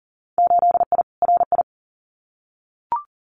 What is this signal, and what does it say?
Morse code, heard as a keyed tone, about 40 wpm, 700 Hz. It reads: 8I RI
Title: Winter Field Day Exchanges Encore 40wpm